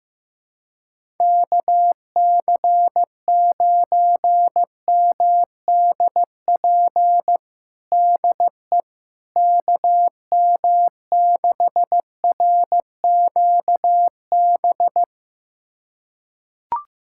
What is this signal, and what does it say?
Morse code, heard as a keyed tone, about 16 wpm, 700 Hz. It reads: KC9MDP DE KM6RQB
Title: USA Callsign DE Callsign - 6 Characters - 15wpm